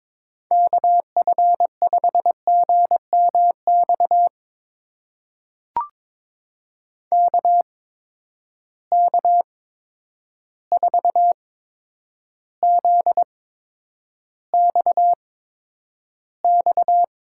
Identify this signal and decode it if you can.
Morse code, heard as a keyed tone, about 22 wpm, spaced as for 9 wpm, 700 Hz. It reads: KF5GMX KK4ZXX
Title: USA Call Signs - 6 Characters - 8x Character Spacing - 22wpm